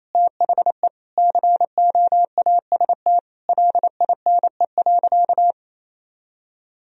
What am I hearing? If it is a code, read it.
Morse code, heard as a keyed tone, about 28 wpm, 700 Hz. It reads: THE COAST LINE.
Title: Sentences from Top 1000 Words - Review Part 4 - 28wpm